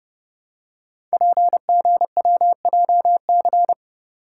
Morse code, heard as a keyed tone, about 30 wpm, 700 Hz. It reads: PGWJC
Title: Code Groups - 5 Letters - 30wpm